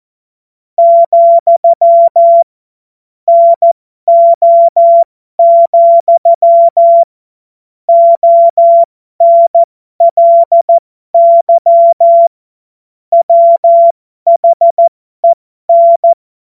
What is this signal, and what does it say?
Morse code, heard as a keyed tone, about 14 wpm, spaced as for 12 wpm, 680 Hz. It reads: , NO, ONLY WHEN
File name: Morsekurs / Witze